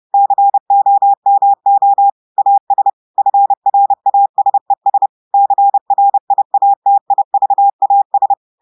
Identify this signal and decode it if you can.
Morse code, heard as a keyed tone, about 30 wpm, 800 Hz. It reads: COMO AS FRASES CRIATIVAS
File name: audios / portugais / portugais texte en clair 2 vitesse 30 mots minutes